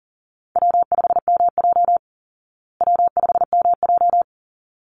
Morse code, heard as a keyed tone, about 40 wpm, 700 Hz. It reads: W5MJ W5MJ
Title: Easy Call Signs Repeated 2x - 40wpm